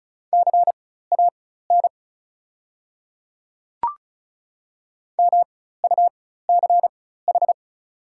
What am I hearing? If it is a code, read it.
Morse code, heard as a keyed tone, about 36 wpm, spaced as for 16 wpm, 700 Hz. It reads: CAN MUCH
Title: 4x Character Spacing - Top 300 Words - 35wpm